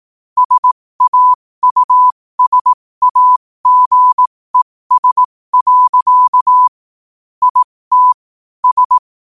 Morse code, heard as a keyed tone, about 18 wpm, spaced as for 15 wpm, 1000 Hz. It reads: SAUSAGES. IT S
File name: qotdfiles / 15wpm